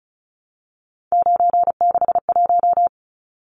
Morse code, heard as a keyed tone, about 34 wpm, 700 Hz. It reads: 961
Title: Numbers - 3-Digit 35wpm